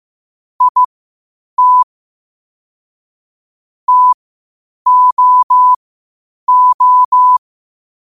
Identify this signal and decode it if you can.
Morse code, heard as a keyed tone, about 15 wpm, spaced as for 8 wpm, 1000 Hz. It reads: IT TOO
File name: qotdfiles / 8wpm